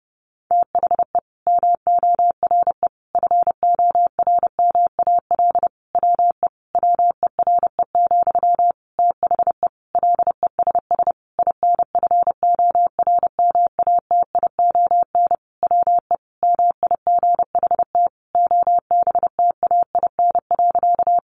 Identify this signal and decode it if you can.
Morse code, heard as a keyed tone, about 30 wpm, 700 Hz. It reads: THE MORE FORMAL WE WERE, THE LESS INFORMATION WE MIGHT OBTAIN.
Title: Sentences from Top 1300 Words 30wpm